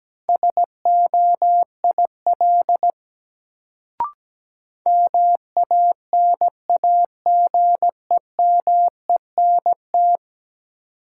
Morse code, heard as a keyed tone, about 17 wpm, 700 Hz. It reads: SOIL MANAGEMENT
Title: Sentences from Top 1000 Words - Review Part 4 - New Word Drill - 17wpm